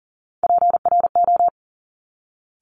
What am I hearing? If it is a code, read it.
Morse code, heard as a keyed tone, about 41 wpm, 690 Hz. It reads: PRO
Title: Top Three Letter Combinations in Words 40wpm